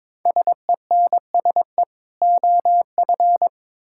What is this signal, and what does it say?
Morse code, heard as a keyed tone, about 22 wpm, 700 Hz. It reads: SENSE OF
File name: Build A Phrase - 2-Word Phrases - 22wpm